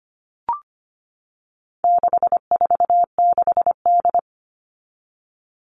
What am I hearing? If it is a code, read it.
Morse code, heard as a keyed tone, about 25 wpm, 700 Hz. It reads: 646D
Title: All Cut Numbers - 4-Digit 25wpm